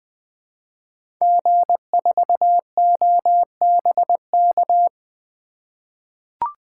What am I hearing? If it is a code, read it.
Morse code, heard as a keyed tone, about 20 wpm, 700 Hz. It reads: G4OBK
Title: 2023 SOTA Chaser Call Signs - 20wpm